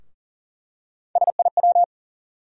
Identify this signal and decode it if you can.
Morse code, heard as a keyed tone, about 40 wpm, 700 Hz. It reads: SIW
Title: USA Call Sign Postfixes - 3 Letters 40wpm